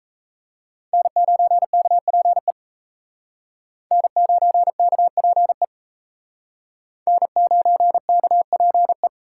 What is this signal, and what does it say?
Morse code, heard as a keyed tone, about 35 wpm, 700 Hz. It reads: N9KPE N9KPE N9KPE
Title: Speed Racing - USA 1x3 Call Signs 28wpm